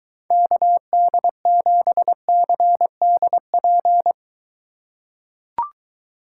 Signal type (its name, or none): Morse code